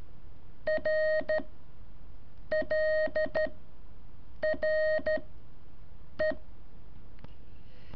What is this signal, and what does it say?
Morse code, heard as a keyed tone, about 13 wpm, 620 Hz. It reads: R L R E
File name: curso / ejercicio 2